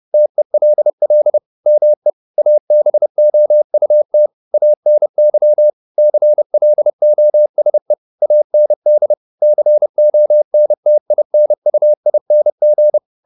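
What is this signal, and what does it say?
Morse code, heard as a keyed tone, about 30 wpm, 600 Hz. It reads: TELL ME ABOUT ANY CLOSE AND CONTINUING
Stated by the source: Morse code at 30 wpm, the novel Never by Ken Follett in English, from CW med Gnister.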